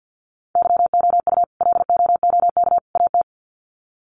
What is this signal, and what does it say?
Morse code, heard as a keyed tone, about 52 wpm, 700 Hz. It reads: YOU LOOK AT
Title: Sets of 3 Words 50wpm